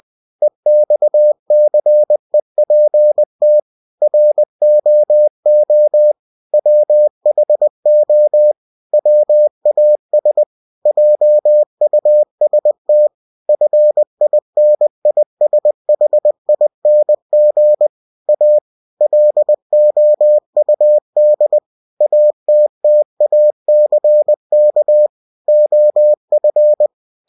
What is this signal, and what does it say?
Morse code, heard as a keyed tone, about 20 wpm, 600 Hz. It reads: EXCEPT ROO WHO WAS JUST FINISHING A LOUD ATTACK OF
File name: cw-books / winnie-the-pooh / winnie-20wpm-page-0107